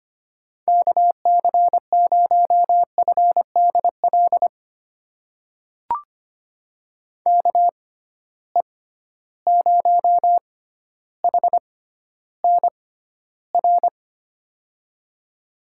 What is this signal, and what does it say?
Morse code, heard as a keyed tone, about 25 wpm, spaced as for 13 wpm, 700 Hz. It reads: KC0FDL KE0HNR
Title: USA Call Signs - 6 Characters - 6x Character Spacing - 25wpm